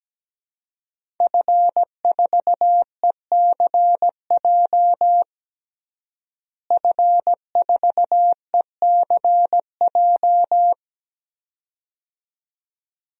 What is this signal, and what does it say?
Morse code, heard as a keyed tone, about 17 wpm, 700 Hz. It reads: F4ECJ F4ECJ
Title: DXCC-100 Call Signs - 5 Characters - Repeated 2x - 17wpm